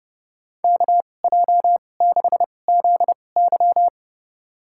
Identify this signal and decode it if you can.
Morse code, heard as a keyed tone, about 30 wpm, spaced as for 22 wpm, 700 Hz. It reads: KJ6ZY
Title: USA Call Signs - 5 Characters - 2x Character Spacing - 30wpm